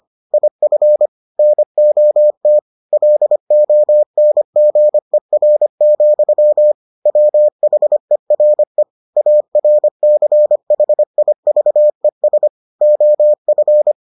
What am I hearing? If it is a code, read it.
Morse code, heard as a keyed tone, about 25 wpm, 600 Hz. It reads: IF NOT LONGER, WHERE ARCHIVES OF